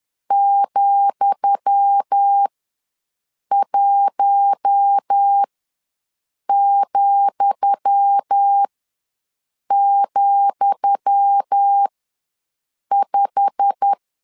Morse code, heard as a keyed tone, about 10 wpm, spaced as for 6 wpm, 800 Hz. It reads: ,1,,5
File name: LEZIONI / 10 SIGNS / 10D COMA ws 10wpm 4314